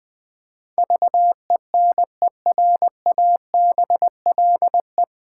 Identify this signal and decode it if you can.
Morse code, heard as a keyed tone, about 20 wpm, 700 Hz. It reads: VENERABLE
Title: Pangram Challenge v2 - 20wpm